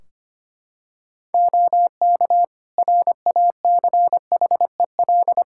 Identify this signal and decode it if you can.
Morse code, heard as a keyed tone, about 25 wpm, 700 Hz. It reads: OK RACHEL
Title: K1USN SST QSO Practice 25wpm